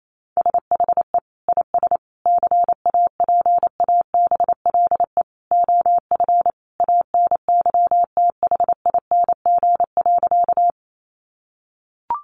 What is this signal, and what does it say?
Morse code, heard as a keyed tone, about 28 wpm, 700 Hz. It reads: SHE IS CAPABLE OF ANYTHING.
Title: Sentences from Top 1200 Words 28wpm